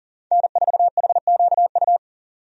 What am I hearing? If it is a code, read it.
Morse code, heard as a keyed tone, about 40 wpm, 700 Hz. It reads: N4HQU
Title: USA Call Signs - 5 Characters - 40wpm